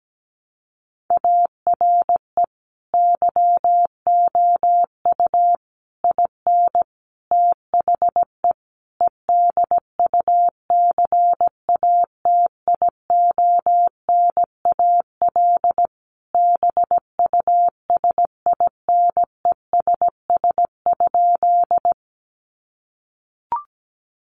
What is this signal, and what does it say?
Morse code, heard as a keyed tone, about 17 wpm, 700 Hz. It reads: ARE YOU IN THE EDUCATIONAL BUSINESS?
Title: Sentences from Top 1200 Words 17wpm